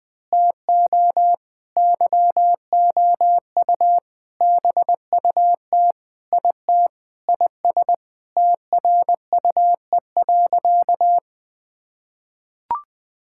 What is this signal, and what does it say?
Morse code, heard as a keyed tone, about 20 wpm, 700 Hz. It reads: TO YOU BUT IT IS TRUE.
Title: Sentences with 8-Word Phrases - 20wpm